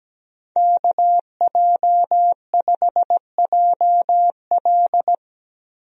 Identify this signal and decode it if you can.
Morse code, heard as a keyed tone, about 17 wpm, 700 Hz. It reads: KJ5JL
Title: USA Call Signs - 5 Characters - Repeated 3x - 17wpm